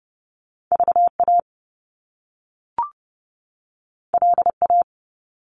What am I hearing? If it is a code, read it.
Morse code, heard as a keyed tone, about 30 wpm, 700 Hz. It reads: VA LA
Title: US States 30wpm